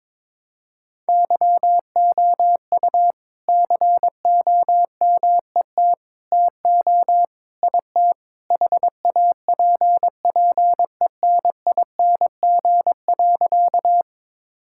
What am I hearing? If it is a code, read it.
Morse code, heard as a keyed tone, about 22 wpm, 700 Hz. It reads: YOU COMET TO IT HAPPENING.
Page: morse-code-ninja-practice-22wpm